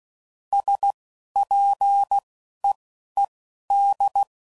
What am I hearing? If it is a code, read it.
Morse code, heard as a keyed tone, about 16 wpm, spaced as for 11 wpm, 800 Hz. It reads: SPEED
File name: CW Code Groups @10WPM / Weather Sets / Weather 5 12